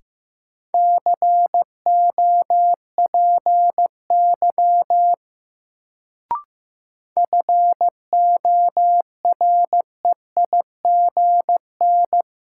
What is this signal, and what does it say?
Morse code, heard as a keyed tone, about 15 wpm, 700 Hz. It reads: COPY FOREIGN
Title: Sentences from Top 1400 Words - New Word Drill Alt - 15wpm